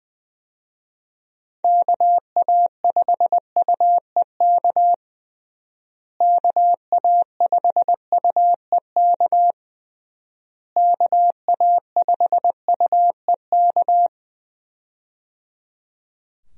Morse code, heard as a keyed tone, about 20 wpm, 700 Hz. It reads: KA5UEK KA5UEK KA5UEK
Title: USA Call Signs - 6 Characters - Repeated 3x - 20wpm